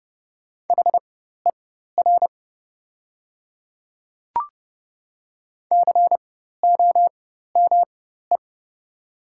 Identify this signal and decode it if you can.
Morse code, heard as a keyed tone, about 30 wpm, spaced as for 14 wpm, 700 Hz. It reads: HER COME
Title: 4x Character Spacing - Top 100 Words - 30wpm